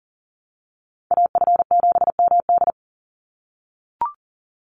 Morse code, heard as a keyed tone, about 40 wpm, 700 Hz. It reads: AF7MD
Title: NA SOTA Chaser Callsigns 40wpm